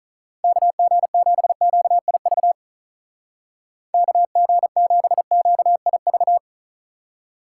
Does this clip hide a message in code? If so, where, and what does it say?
Yes: Morse code from 0.4 to 6.4 s, heard as a keyed tone, about 45 wpm, 700 Hz. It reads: KG7QIV KG7QIV